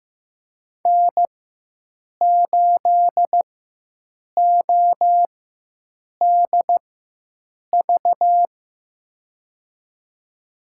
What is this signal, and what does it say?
Morse code, heard as a keyed tone, about 15 wpm, spaced as for 7 wpm, 700 Hz. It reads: N8ODV
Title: USA Call Signs - 4x Character Spacing - 15wpm